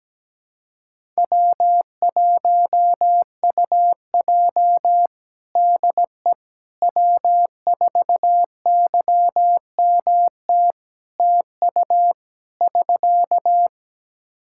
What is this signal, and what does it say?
Morse code, heard as a keyed tone, about 17 wpm, 700 Hz. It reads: W1UJ DE W4YMT TU <SK>
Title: Practice QSO - Exchange 4 - Ending - 17wpm